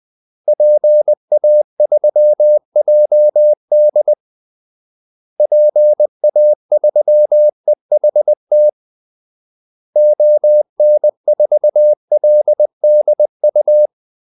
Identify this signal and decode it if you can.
Morse code, heard as a keyed tone, about 20 wpm, 600 Hz. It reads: PA3JD PA3EHT ON4LDU
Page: Kaldesignaler 20 wpm